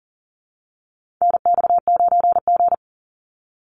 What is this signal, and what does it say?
Morse code, heard as a keyed tone, about 40 wpm, 700 Hz. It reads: NX9G
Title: Call Signs - 40wpm